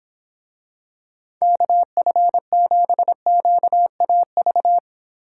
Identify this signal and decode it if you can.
Morse code, heard as a keyed tone, about 26 wpm, 700 Hz. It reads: KF7QAV